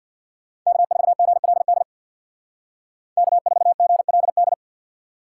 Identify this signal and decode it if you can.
Morse code, heard as a keyed tone, about 61 wpm, 700 Hz. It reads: K4GLD K4GLD
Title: Speed Racing - USA 1x3 Call Signs 50wpm